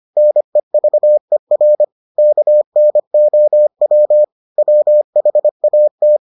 Morse code, heard as a keyed tone, about 25 wpm, 600 Hz. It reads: NEVER KNOW WHAT